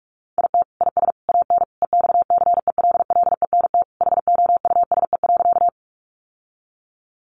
Morse code, heard as a keyed tone, about 44 wpm, 700 Hz. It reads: IT IS AN EXCELLENT HOUSE.
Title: Sentences from Top 1000 Words 45wpm